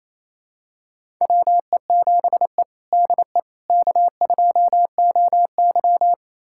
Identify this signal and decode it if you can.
Morse code, heard as a keyed tone, about 28 wpm, 700 Hz. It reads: WE7E DE K2OY
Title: USA Callsign DE Callsign - 4 Characters - 28wpm